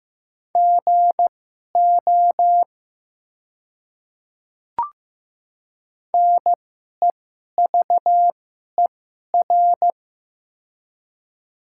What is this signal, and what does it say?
Morse code, heard as a keyed tone, about 15 wpm, spaced as for 11 wpm, 700 Hz. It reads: GO NEVER
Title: 2x Character Spacing - Top 300 Words - 15wpm